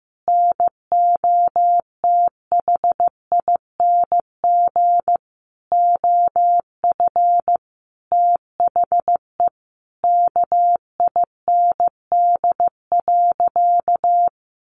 Morse code, heard as a keyed tone, about 14 wpm, 700 Hz. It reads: NOTHING OF THE KIND.
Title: Sentences from Top 1200 Words 15wpm